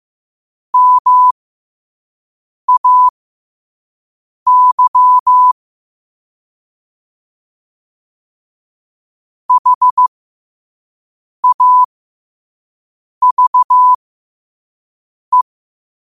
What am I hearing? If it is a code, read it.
Morse code, heard as a keyed tone, about 15 wpm, spaced as for 5 wpm, 1000 Hz. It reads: MAY HAVE
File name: qotdfiles / 5wpm